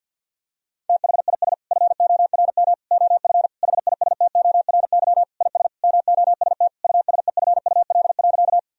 Morse code, heard as a keyed tone, about 50 wpm, 700 Hz. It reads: THIS FORM OF HISTORY IS MOST USEFUL.
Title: Sentences from Top 1000 Words - Review Part 4 - 50wpm